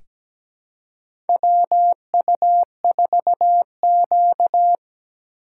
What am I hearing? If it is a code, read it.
Morse code, heard as a keyed tone, about 17 wpm, 700 Hz. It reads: WU4Q
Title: USA Call Signs - 4 Characters - 4x Character Spacing - 17wpm